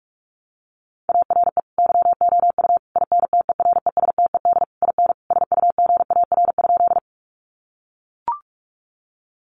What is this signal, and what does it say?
Morse code, heard as a keyed tone, about 44 wpm, 700 Hz. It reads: ARE YOU INTERESTED IN SUGAR?
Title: Sentences from Top 1200 Words 45wpm